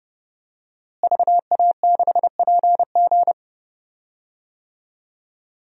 Morse code, heard as a keyed tone, about 30 wpm, 700 Hz. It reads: VA6PG
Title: DXCC-100 Call Signs - 5 Characters - Repeated 2x - 30wpm